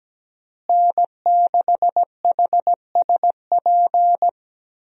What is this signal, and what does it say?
Morse code, heard as a keyed tone, about 17 wpm, 700 Hz. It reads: N6HSP
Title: USA Call Signs - 5 Characters - Repeated 2x - 17wpm